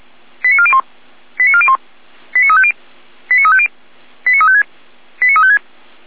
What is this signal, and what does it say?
Five-tone selective call (ZVEI), heard as quick runs of five tones, tones of about 70 ms. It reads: 89401 89401 89350 89350 89356 89356